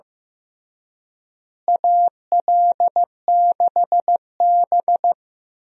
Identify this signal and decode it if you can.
Morse code, heard as a keyed tone, about 15 wpm, 700 Hz. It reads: AL6B
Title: USA Call Signs - 4 Characters - Repeated 3x - 15wpm